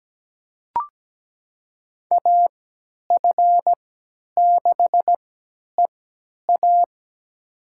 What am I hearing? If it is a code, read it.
Morse code, heard as a keyed tone, about 17 wpm, spaced as for 10 wpm, 700 Hz. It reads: AF6EA
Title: USA Call Signs - 5 Characters - 3x Character Spacing - 17wpm